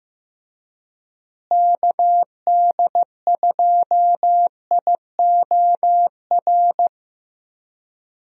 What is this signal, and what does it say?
Morse code, heard as a keyed tone, about 15 wpm, 700 Hz. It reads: KD2IOR
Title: USA Call Signs - 6 Characters - Repeated 2x - 15wpm